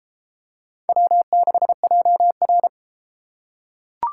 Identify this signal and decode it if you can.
Morse code, heard as a keyed tone, about 31 wpm, 700 Hz. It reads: W6JR